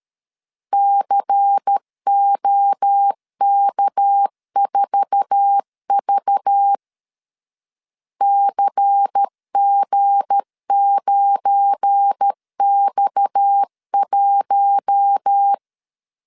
Morse code, heard as a keyed tone, about 13 wpm, 800 Hz. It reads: COK4V CG9X1
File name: LEZIONI / 9 2 12wpm 2301